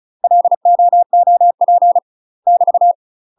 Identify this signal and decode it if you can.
Morse code, heard as a keyed tone, about 34 wpm, 700 Hz. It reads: LOOP <BT>
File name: UK MASTER QSOs TRAINING / 35WPM / UK MASTER QSOs 0043